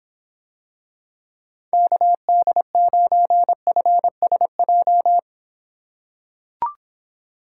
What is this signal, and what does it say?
Morse code, heard as a keyed tone, about 25 wpm, 700 Hz. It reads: KD9FSJ